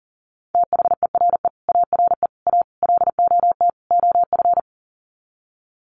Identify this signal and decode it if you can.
Morse code, heard as a keyed tone, about 40 wpm, 700 Hz. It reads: THERE ARE A LOT OF
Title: Sets of 5 Words 40wpm